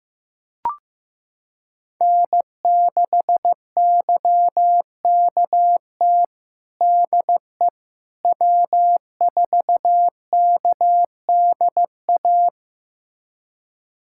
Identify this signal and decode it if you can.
Morse code, heard as a keyed tone, about 15 wpm, 700 Hz. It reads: N6YKT DE W4KDA
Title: USA Callsign DE Callsign - 5 Characters - 15wpm